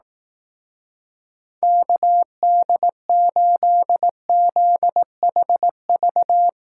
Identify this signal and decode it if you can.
Morse code, heard as a keyed tone, about 18 wpm, 700 Hz. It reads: KD8ZHV
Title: Speed Racing - USA 2x3 Call Signs 15wpm